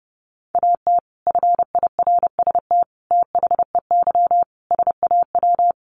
Morse code, heard as a keyed tone, about 30 wpm, 700 Hz. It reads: AT FIRST THEY SAW
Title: Sentences from Top 600 Words 30wpm